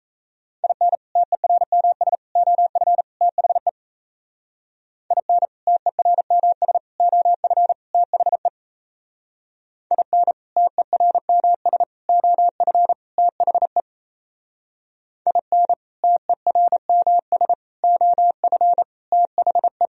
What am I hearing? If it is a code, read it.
Morse code, heard as a keyed tone, about 35 wpm, 700 Hz. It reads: IN TERMS OF THE IN TERMS OF THE IN TERMS OF THE IN TERMS OF THE